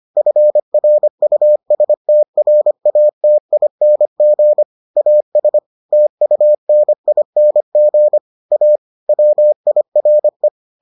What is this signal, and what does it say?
Morse code, heard as a keyed tone, about 25 wpm, 600 Hz. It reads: FRUSTRATING AS TUNING A WIRE